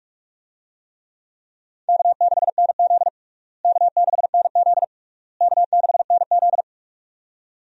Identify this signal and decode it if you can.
Morse code, heard as a keyed tone, about 46 wpm, 700 Hz. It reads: K6NZ K6NZ K6NZ
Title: USA Call Signs - 4 Characters - Repeated 3x - 45wpm